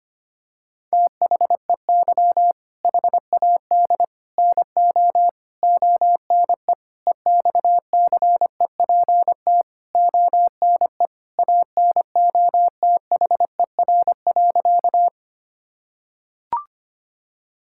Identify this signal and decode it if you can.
Morse code, heard as a keyed tone, about 25 wpm, 700 Hz. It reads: THEY HAD NO ONE EXCEPT ONE ANOTHER.
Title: Sentences from Top 1600 Words 25wpm